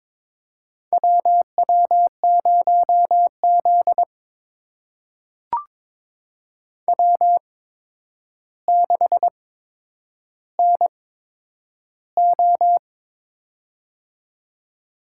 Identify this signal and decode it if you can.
Morse code, heard as a keyed tone, about 22 wpm, spaced as for 10 wpm, 700 Hz. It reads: WW0Z W6NO
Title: USA Call Signs - 4 Characters - 8x Character Spacing - 22wpm